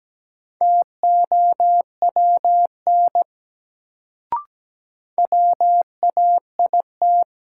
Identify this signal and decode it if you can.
Morse code, heard as a keyed tone, about 17 wpm, 700 Hz. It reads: TOWN WAIT
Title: Sentences from Top 500 Words - New Word Drill - 17wpm